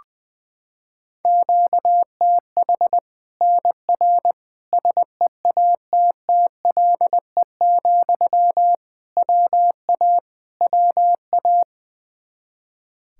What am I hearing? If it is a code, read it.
Morse code, heard as a keyed tone, about 20 wpm, 700 Hz. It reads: QTH NR SEATTLE, WA WA
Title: Practice QSO - Exchange 1 - Beginning - 20wpm